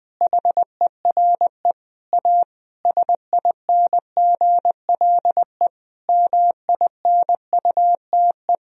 Morse code, heard as a keyed tone, about 20 wpm, 700 Hz. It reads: HERE A SINGLE MINUTE
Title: Sentences from Top 1000 Words - Review Part 4 - 20wpm